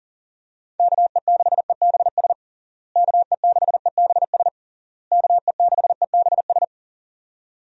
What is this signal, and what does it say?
Morse code, heard as a keyed tone, about 40 wpm, 700 Hz. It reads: KE6EBS KE6EBS KE6EBS
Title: USA Call Signs - 6 Characters - Repeated 3x - 40wpm